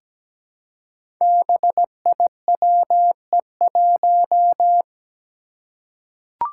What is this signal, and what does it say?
Morse code, heard as a keyed tone, about 17 wpm, 700 Hz. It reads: BIWE1
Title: Five Challenging Letter-Number Combinations 17wpm